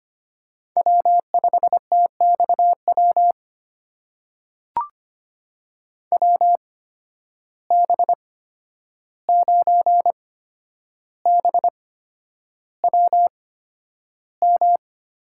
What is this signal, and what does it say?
Morse code, heard as a keyed tone, about 25 wpm, spaced as for 10 wpm, 700 Hz. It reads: W5TXW WB9BWM
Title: USA Call Signs - 8x Character Spacing - 25wpm